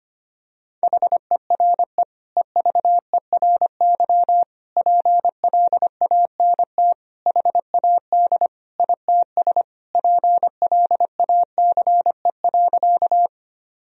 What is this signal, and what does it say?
Morse code, heard as a keyed tone, about 25 wpm, 700 Hz. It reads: HERE EVERY PLANT HAD ITS PLACE.
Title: Sentences from Top 1000 Words - Review Part 1 - 25wpm